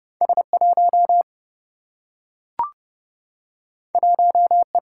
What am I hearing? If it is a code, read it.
Morse code, heard as a keyed tone, about 30 wpm, 700 Hz. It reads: S 1 1 E
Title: Single-Letter Single-Number 30wpm